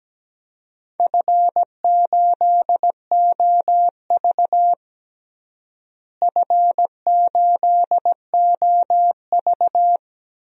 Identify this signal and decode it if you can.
Morse code, heard as a keyed tone, about 17 wpm, 700 Hz. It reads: F8OV F8OV
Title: DXCC-100 Call Signs - 4 Characters - Repeated 3x - 17wpm